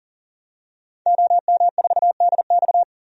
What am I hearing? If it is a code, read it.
Morse code, heard as a keyed tone, about 38 wpm, 700 Hz. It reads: OM4DX
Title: DXCC-100 Call Signs - 5 Characters - 40wpm